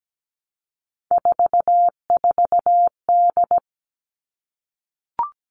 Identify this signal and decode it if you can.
Morse code, heard as a keyed tone, about 17 wpm, 700 Hz. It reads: 44D
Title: All Cut Numbers - 3-Digit 17wpm